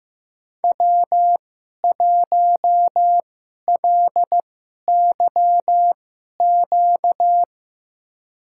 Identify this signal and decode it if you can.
Morse code, heard as a keyed tone, about 15 wpm, spaced as for 11 wpm, 700 Hz. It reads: W1LYQ
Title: USA Call Signs - 5 Characters - 2x Character Spacing - 15wpm